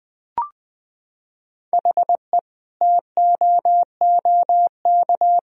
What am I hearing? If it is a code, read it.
Morse code, heard as a keyed tone, about 20 wpm, 700 Hz. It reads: HE TOOK
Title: Sentences from Top 1000 Words - Review Part 2 - 20wpm